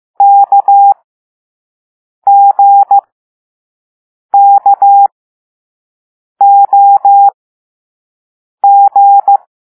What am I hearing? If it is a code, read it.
Morse code, heard as a keyed tone, about 15 wpm, spaced as for 5 wpm, 800 Hz. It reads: KGKOG
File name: Morse / Level4A